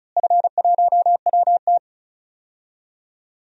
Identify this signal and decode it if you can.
Morse code, heard as a keyed tone, about 34 wpm, 700 Hz. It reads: F1WT
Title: DXCC-100 Call Signs - 4 Characters - Repeated 3x - 35wpm